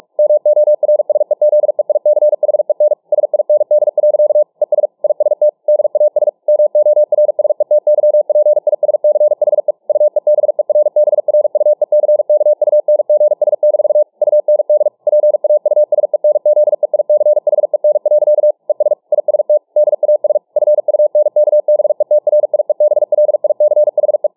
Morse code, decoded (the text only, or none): MORSEZEICHEN SIND. ES IST DAS MORSETYPISCHE UEBERBRUECKUNGS- UND PAUSENZEICHEN. ES IST DAS FUNKBETRIEBLICHE